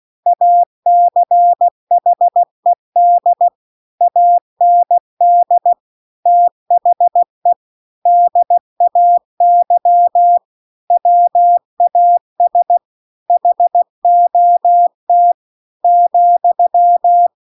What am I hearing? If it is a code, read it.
Morse code, heard as a keyed tone, about 16 wpm, 700 Hz. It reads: ACHED AND THE DAY WAS HOT ,